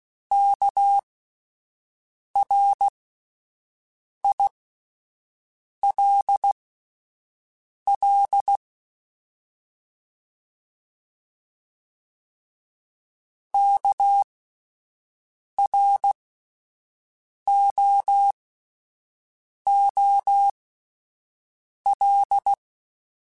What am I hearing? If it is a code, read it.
Morse code, heard as a keyed tone, about 16 wpm, spaced as for 5 wpm, 800 Hz. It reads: KRILL KROOL
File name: Alphabetical / Alphabetical mp3s / Ks @5